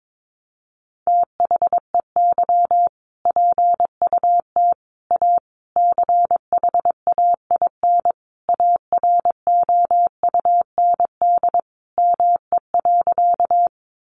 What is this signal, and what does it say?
Morse code, heard as a keyed tone, about 22 wpm, 700 Hz. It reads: THEY PUT A CHAIN AROUND ME.
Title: Sentences from Top 1500 Words 22wpm